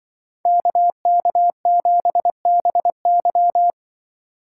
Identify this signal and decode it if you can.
Morse code, heard as a keyed tone, about 24 wpm, 700 Hz. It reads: KK7BY
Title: Speed Racing - USA 2x2 Call Signs 20wpm